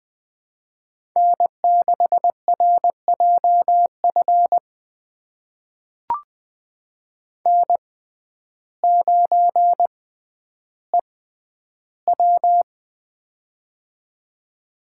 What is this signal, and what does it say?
Morse code, heard as a keyed tone, about 20 wpm, spaced as for 11 wpm, 700 Hz. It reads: N6RJF N9EW
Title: USA Call Signs - 6x Character Spacing - 20wpm